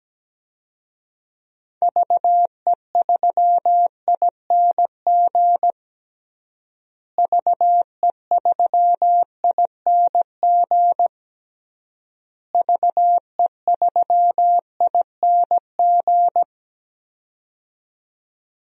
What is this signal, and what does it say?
Morse code, heard as a keyed tone, about 17 wpm, 700 Hz. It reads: VE3ING VE3ING VE3ING
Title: DXCC-100 Call Signs - 6 Characters - Repeated 3x - 17wpm